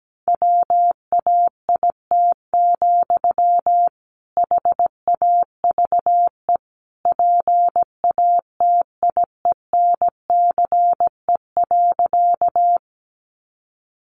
Morse code, heard as a keyed tone, about 17 wpm, 700 Hz. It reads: WAIT, HAVE PATIENCE.
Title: Sentences from Top 1300 Words 17wpm